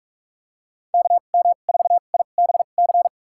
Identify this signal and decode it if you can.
Morse code, heard as a keyed tone, about 47 wpm, spaced as for 33 wpm, 700 Hz. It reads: KM4IBC
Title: USA Call Signs - 2x Character Spacing - 45wpm